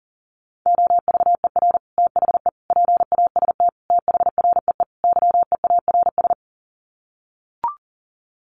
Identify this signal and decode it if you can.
Morse code, heard as a keyed tone, about 40 wpm, 700 Hz. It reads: OVER THE PAST THREE YEARS
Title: Sets of 5 Words Encore2 40wpm